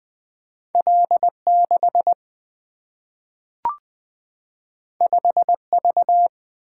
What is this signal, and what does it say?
Morse code, heard as a keyed tone, about 20 wpm, 700 Hz. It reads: L6 5V
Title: Single-Letter Single-Number 20wpm